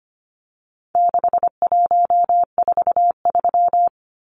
Morse code, heard as a keyed tone, about 25 wpm, 700 Hz. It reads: 6143
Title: Numbers - 4-Digit 25wpm